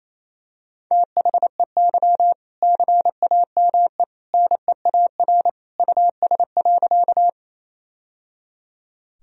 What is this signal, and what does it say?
Morse code, heard as a keyed tone, about 28 wpm, 700 Hz. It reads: THEY CAME NEAR US.
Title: Sentences from Top 1000 Words - Review Part 1 - 28wpm